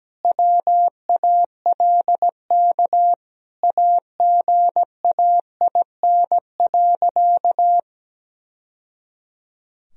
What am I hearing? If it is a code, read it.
Morse code, heard as a keyed tone, about 17 wpm, 700 Hz. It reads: WALK AGAIN.
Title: Sentences from Top 1000 Words - Review Part 1 - 17wpm